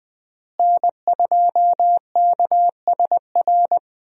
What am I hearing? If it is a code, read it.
Morse code, heard as a keyed tone, about 21 wpm, 700 Hz. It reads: N2KSR
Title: USA Call Signs - 5 Characters - Repeated 3x - 20wpm